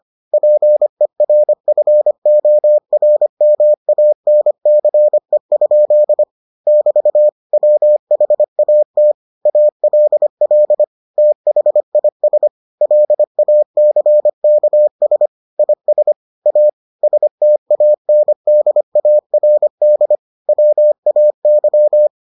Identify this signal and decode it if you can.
Morse code, heard as a keyed tone, about 25 wpm, 600 Hz. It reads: PERFORMANCE? <BT> WHAT ALL THIS LACKS IS A STANDARD WAY